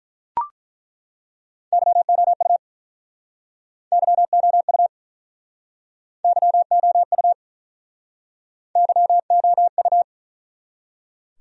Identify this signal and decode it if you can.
Morse code, heard as a keyed tone, about 49 wpm, 700 Hz. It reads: YOU YOU YOU YOU